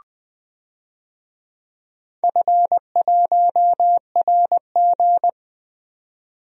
Morse code, heard as a keyed tone, about 20 wpm, 700 Hz. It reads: F1RG
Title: DXCC-100 Call Signs - 4 Characters - 20wpm